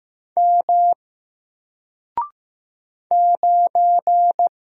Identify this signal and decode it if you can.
Morse code, heard as a keyed tone, about 15 wpm, 700 Hz. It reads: M 9
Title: Single Character (Letter:Number:Punctuation:Prosign) Alt 15wpm